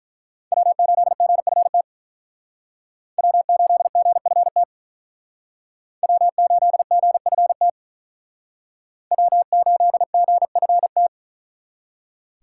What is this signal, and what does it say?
Morse code, heard as a keyed tone, about 43 wpm, 700 Hz. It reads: W8GFT W8GFT W8GFT W8GFT
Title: Speed Racing - USA 1x3 Call Signs 35wpm